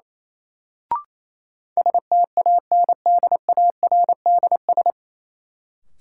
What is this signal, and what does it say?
Morse code, heard as a keyed tone, about 27 wpm, 700 Hz. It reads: STANDARDS
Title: Sentences from Top 1000 Words - Review Part 6 - New Word Drill - 28wpm